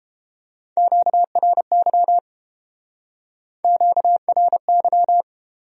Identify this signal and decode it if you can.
Morse code, heard as a keyed tone, about 30 wpm, 700 Hz. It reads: QRY QRY